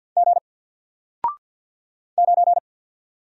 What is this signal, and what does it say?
Morse code, heard as a keyed tone, about 50 wpm, 700 Hz. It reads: G 9
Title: Single Letter-Number 50wpm